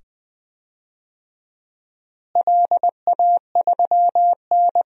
Morse code, heard as a keyed tone, about 20 wpm, 700 Hz. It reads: LA3N
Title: DXCC-100 Call Signs - 4 Characters - Repeated 2x - 20wpm